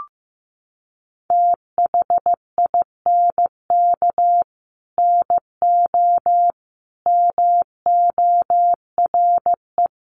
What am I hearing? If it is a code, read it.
Morse code, heard as a keyed tone, about 15 wpm, 700 Hz. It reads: THINK NO MORE
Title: Sentences from Top 200 Words 15wpm